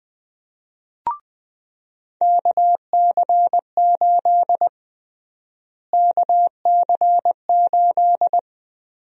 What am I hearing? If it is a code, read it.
Morse code, heard as a keyed tone, about 20 wpm, 700 Hz. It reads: KC8 KC8
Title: USA Call Sign Prefixes - Repeated 2x - 20wpm